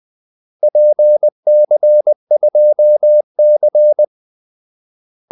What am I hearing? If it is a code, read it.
Morse code, heard as a keyed tone, about 20 wpm, 600 Hz. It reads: PC2C
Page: Kaldesignaler 20 wpm